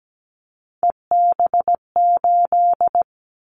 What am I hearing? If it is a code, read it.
Morse code, heard as a keyed tone, about 17 wpm, 700 Hz. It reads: EB8
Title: Callsign Prefixes for Top 50 DXCC - 17wpm